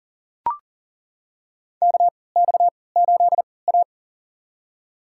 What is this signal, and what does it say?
Morse code, heard as a keyed tone, about 40 wpm, spaced as for 23 wpm, 700 Hz. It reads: KX8A
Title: USA Call Signs - 4 Characters - 3x Character Spacing - 40wpm